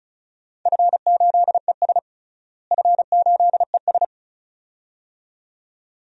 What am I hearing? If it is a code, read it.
Morse code, heard as a keyed tone, about 34 wpm, 690 Hz. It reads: F8ES F8ES
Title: DXCC-100 Call Signs - 4 Characters - Repeated 2x - 35wpm